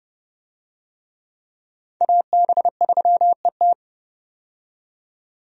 Morse code, heard as a keyed tone, about 31 wpm, 700 Hz. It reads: AB3ET
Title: USA Call Signs - 5 Characters - 30wpm